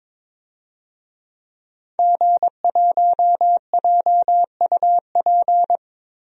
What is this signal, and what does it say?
Morse code, heard as a keyed tone, about 22 wpm, 700 Hz. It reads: G1JUP
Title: DXCC-100 Call Signs - 5 Characters - Repeated 3x - 22wpm